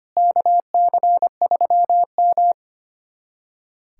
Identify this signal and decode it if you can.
Morse code, heard as a keyed tone, about 25 wpm, 700 Hz. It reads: KC3M
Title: USA Callsign DE Callsign - 4 Characters - 25wpm